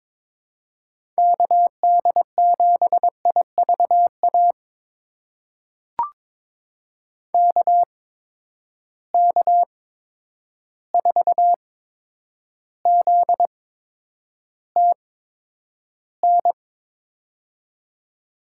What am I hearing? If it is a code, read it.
Morse code, heard as a keyed tone, about 22 wpm, spaced as for 9 wpm, 700 Hz. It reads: KD7IVA KK4ZTN
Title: USA Call Signs - 6 Characters - 8x Character Spacing - 22wpm